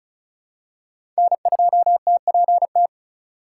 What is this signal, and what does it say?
Morse code, heard as a keyed tone, about 34 wpm, 700 Hz. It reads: N2TPT
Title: USA Call Signs - 4x Character Spacing - 35wpm